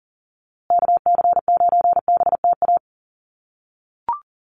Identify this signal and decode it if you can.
Morse code, heard as a keyed tone, about 40 wpm, 700 Hz. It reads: KC9BTA
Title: USA Call Signs Repeated 2x 40wpm